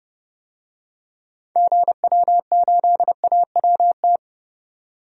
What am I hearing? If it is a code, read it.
Morse code, heard as a keyed tone, about 30 wpm, 700 Hz. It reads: GW8AWT
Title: DXCC-100 Call Signs - 6 Characters - 30wpm